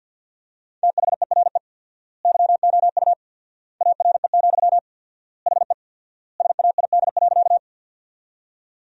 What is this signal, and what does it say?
Morse code, heard as a keyed tone, about 50 wpm, 700 Hz. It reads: THERE YOU ARE, HE SAID.
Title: 4x Word-Spacing - Sentences from Top 100 Words - 50wpm